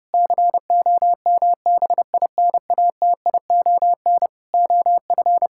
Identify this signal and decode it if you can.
Morse code, heard as a keyed tone, about 30 wpm, 700 Hz. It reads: COMBINATION OF
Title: Sentences from Top 1000 Words - Review Part 6 - 30wpm